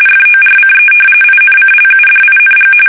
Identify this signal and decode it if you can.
RTTY, about 45.45 baud. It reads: FORCE INCLUDING
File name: fsk850 / battleshipnj